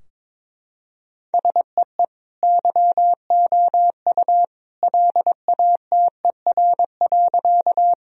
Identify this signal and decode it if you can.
Morse code, heard as a keyed tone, about 22 wpm, 700 Hz. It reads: SEE YOU LATER.
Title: Sentences from Top 1000 Words - Review Part 4 - 22wpm